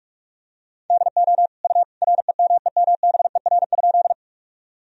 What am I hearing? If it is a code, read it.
Morse code, heard as a keyed tone, about 43 wpm, 700 Hz. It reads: DO U REMEMBER?
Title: Sentences from Top 400 Words and Abbreviations 45wpm